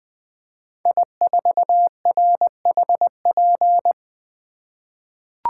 Morse code, heard as a keyed tone, about 20 wpm, 700 Hz. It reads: I4RHP
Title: 2023 POTA Hunter Call Signs - 20wpm